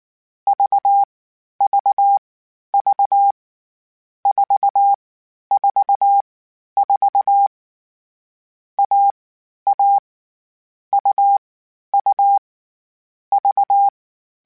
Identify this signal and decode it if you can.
Morse code, heard as a keyed tone, about 19 wpm, 800 Hz. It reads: V V V 4 4 4 A A U U V